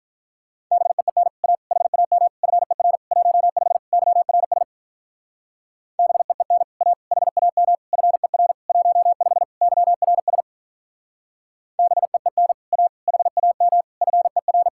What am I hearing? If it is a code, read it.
Morse code, heard as a keyed tone, about 45 wpm, 700 Hz. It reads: BEEN A HAM FER 15 YRS BEEN A HAM FER 15 YRS BEEN A HAM FER